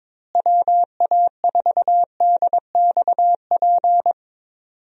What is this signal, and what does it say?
Morse code, heard as a keyed tone, about 21 wpm, 700 Hz. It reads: WA4DXP
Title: USA Call Signs - 6 Characters - Repeated 3x - 22wpm